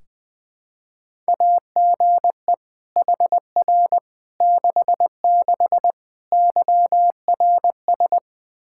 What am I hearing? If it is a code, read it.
Morse code, heard as a keyed tone, about 21 wpm, 700 Hz. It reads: AGE HR 66 YRS
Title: Practice QSO - Exchange 3 - More Info - 20wpm